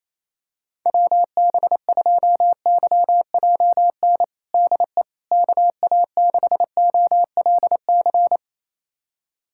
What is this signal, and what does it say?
Morse code, heard as a keyed tone, about 28 wpm, 700 Hz. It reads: WB2YJN DE KA6OLC
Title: USA Callsign DE Callsign - 6 Characters - 28wpm